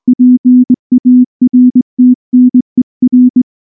Morse code, heard as a keyed tone, about 20 wpm, 250 Hz. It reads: PARTNER
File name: morse / morse-training-20WPM-2